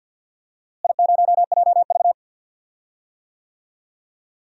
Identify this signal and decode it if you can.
Morse code, heard as a keyed tone, about 46 wpm, 700 Hz. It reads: I0JV
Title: DXCC-100 Call Signs - 4 Characters - Repeated 2x - 50wpm